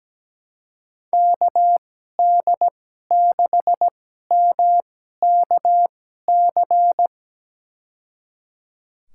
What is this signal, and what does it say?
Morse code, heard as a keyed tone, about 17 wpm, spaced as for 12 wpm, 700 Hz. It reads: KD6MKC
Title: USA Call Signs - 6 Characters - 2x Character Spacing - 17wpm